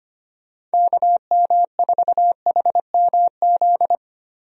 Morse code, heard as a keyed tone, about 25 wpm, 700 Hz. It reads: KM4HMZ
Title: USA Call Signs - 6 Characters - Repeated 3x - 25wpm